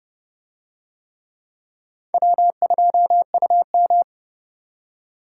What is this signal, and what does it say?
Morse code, heard as a keyed tone, about 29 wpm, 700 Hz. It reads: W2UM